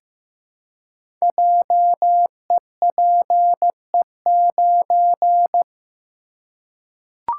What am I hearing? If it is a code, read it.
Morse code, heard as a keyed tone, about 15 wpm, 700 Hz. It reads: JEPE9
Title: Code Groups - 5 Letter:Numbers - 15wpm